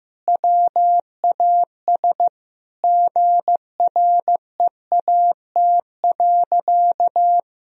Morse code, heard as a keyed tone, about 16 wpm, 700 Hz. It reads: WAS GREAT.
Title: Sentences from Top 1000 Words - Review Part 4 - 15wpm